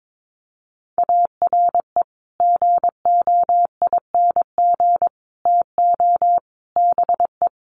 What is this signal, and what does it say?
Morse code, heard as a keyed tone, about 22 wpm, 700 Hz. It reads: ARE GOING TO BE
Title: Sets of 4 Words 22wpm